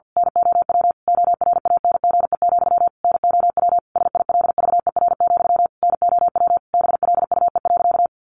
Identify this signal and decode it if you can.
Morse code, heard as a keyed tone, about 50 wpm, 700 Hz. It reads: NOW ORANGE, NOW SILVER, NOW BLUE.
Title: A Sound Of Thunder 50wpm